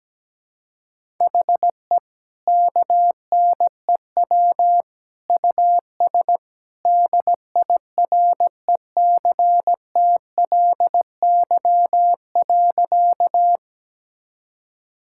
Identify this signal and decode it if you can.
Morse code, heard as a keyed tone, about 17 wpm, 700 Hz. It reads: HE KNEW US DIRECTLY.
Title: Sentences from Top 1000 Words - Review Part 5 - 17wpm